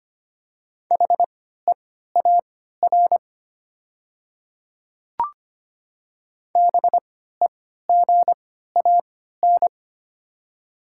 Morse code, heard as a keyed tone, about 25 wpm, spaced as for 14 wpm, 700 Hz. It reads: HEAR BEGAN
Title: 3x Character Spacing - Top 300 Words - 25wpm